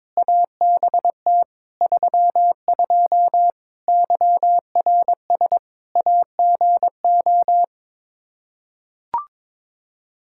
Morse code, heard as a keyed tone, about 22 wpm, 700 Hz. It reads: ABT 32 YRS AGO
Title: Practice QSO - Exchange 3 - More Info - 22wpm